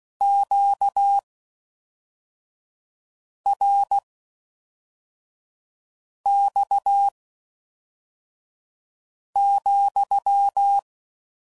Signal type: Morse code